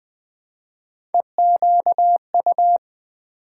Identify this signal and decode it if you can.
Morse code, heard as a keyed tone, about 20 wpm, 700 Hz. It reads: EQU
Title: Build A Word - 1 Letter At A Time - 5-Letter Words - 20wpm